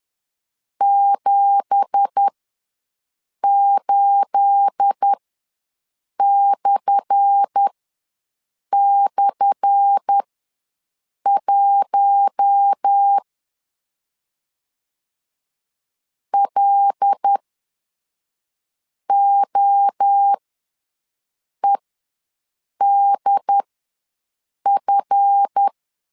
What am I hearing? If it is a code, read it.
Morse code, heard as a keyed tone, about 10 wpm, spaced as for 5 wpm, 800 Hz. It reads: 78//1 LOEDF
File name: LEZIONI / 10 SIGNS / 10A SLASH ws 10wpm 4121